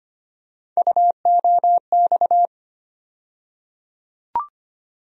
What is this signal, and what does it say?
Morse code, heard as a keyed tone, about 25 wpm, 700 Hz. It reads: UOX
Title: USA Call Sign Postfixes - 3 Letters 25wpm